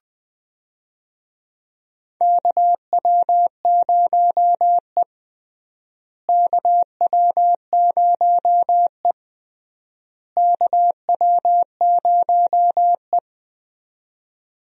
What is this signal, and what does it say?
Morse code, heard as a keyed tone, about 20 wpm, 700 Hz. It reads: KW0E KW0E KW0E
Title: USA Call Signs - 4 Characters - Repeated 3x - 20wpm